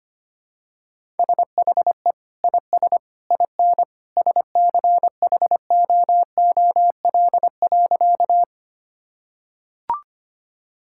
Morse code, heard as a keyed tone, about 25 wpm, 690 Hz. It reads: SHE IS IN SCHOOL.
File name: practice / 25wpm / Sentences with 2-Word Phrases - 25wpm